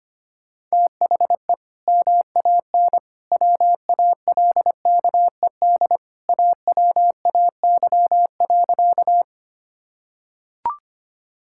Morse code, heard as a keyed tone, about 25 wpm, 700 Hz. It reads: THE MAN WALKED AWAY.
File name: practice / 25wpm / Sentences with 2-Word Phrases - 25wpm